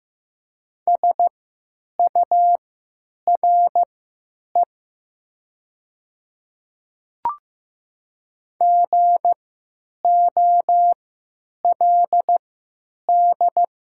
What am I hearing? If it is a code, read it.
Morse code, heard as a keyed tone, about 15 wpm, spaced as for 9 wpm, 700 Hz. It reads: SURE GOLD
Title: 3x Character Spacing - Top 500 Words - 15wpm